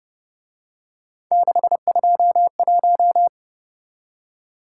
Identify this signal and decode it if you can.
Morse code, heard as a keyed tone, about 28 wpm, 700 Hz. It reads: 621